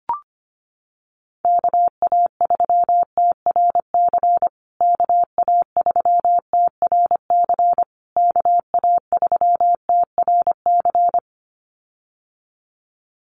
Morse code, heard as a keyed tone, about 25 wpm, 700 Hz. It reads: KA3TRC KA3TRC KA3TRC
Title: USA Call Signs Repeated 3x 25wpm